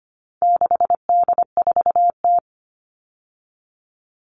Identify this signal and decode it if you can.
Morse code, heard as a keyed tone, about 25 wpm, 700 Hz. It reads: 6D4T
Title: All Cut Numbers - 4-Digit 25wpm